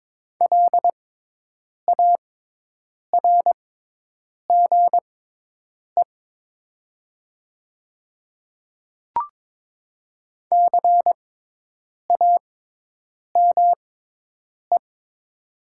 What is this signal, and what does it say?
Morse code, heard as a keyed tone, about 22 wpm, spaced as for 8 wpm, 700 Hz. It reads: LARGE CAME
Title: 6x Character Spacing - Top 300 Words - 22wpm